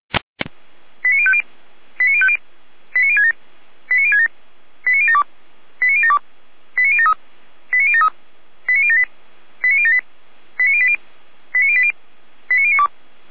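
Five-tone selective call (ZVEI), heard as quick runs of five tones, tones of about 70 ms. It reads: 89050 89050 89067 89067 89072 89072 89073 89073 89078 89078 89080 89080 89092
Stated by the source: ZVEI five-tone selective call received with an SDR.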